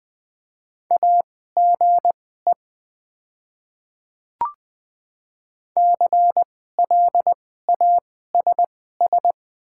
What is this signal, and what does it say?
Morse code, heard as a keyed tone, about 20 wpm, spaced as for 14 wpm, 690 Hz. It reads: AGE CLASS
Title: 2x Character Spacing - Top 500 Words - 20wpm